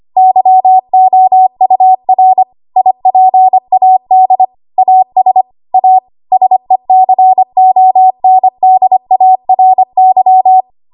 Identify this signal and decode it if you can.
Morse code, heard as a keyed tone, about 25 wpm, 750 Hz. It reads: YOUR IPAD AS A SECONDARY
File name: Archive / 25 WPM / 25WPM